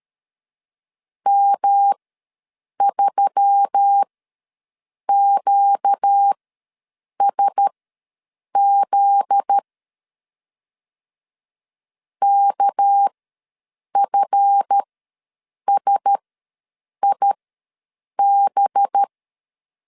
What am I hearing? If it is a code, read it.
Morse code, heard as a keyed tone, about 13 wpm, spaced as for 7 wpm, 800 Hz. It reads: M3QSZ KFSIB
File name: LEZIONI / 7 GF9Z / 7 GF9Z ws 12wpm 3321